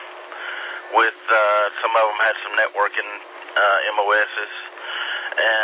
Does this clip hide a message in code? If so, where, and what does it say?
Yes: Emergency Alert System (SAME) tones from 0.3 to 4.5 s, heard as harsh buzzing bursts, with speech.